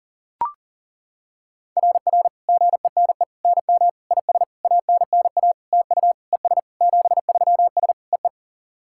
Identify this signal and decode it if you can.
Morse code, heard as a keyed tone, about 40 wpm, 700 Hz. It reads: RR GENE NM IS ANNA TU ES 73S EE
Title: SOTA QSO Practice 40wpm